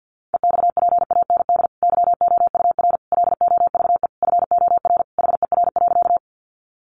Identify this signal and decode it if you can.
Morse code, heard as a keyed tone, about 50 wpm, 700 Hz. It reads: EXPAND YOUR LOVE FOR HER.
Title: Sentences from Top 1400 Words 50wpm